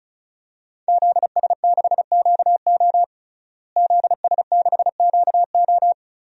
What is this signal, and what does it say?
Morse code, heard as a keyed tone, about 36 wpm, 700 Hz. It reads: ZS6QO ZS6QO
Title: DXCC-100 Call Signs - 5 Characters - Repeated 3x - 35wpm